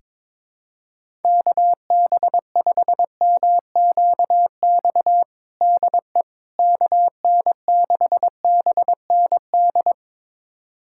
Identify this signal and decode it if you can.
Morse code, heard as a keyed tone, about 22 wpm, 700 Hz. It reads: KB5MQX DE KN6BND
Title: USA Callsign DE Callsign 22wpm